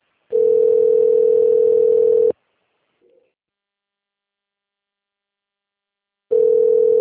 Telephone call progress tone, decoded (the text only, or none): ringback tone